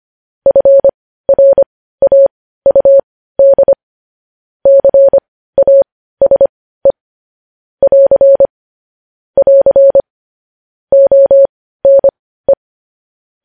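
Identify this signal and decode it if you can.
Morse code, heard as a keyed tone, about 25 wpm, spaced as for 15 wpm, 550 Hz. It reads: FRAUD CASE <AR> <AR> ONE
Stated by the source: Morse code news headlines sent at 15, preceded by vvv vvv.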